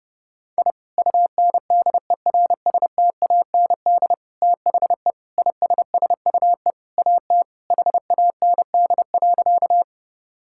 Morse code, heard as a keyed tone, about 30 wpm, 700 Hz. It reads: I UNDERSTAND THE ISSUE AT HAND.
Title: Sentences from Top 1000 Words - Review Part 6 - 30wpm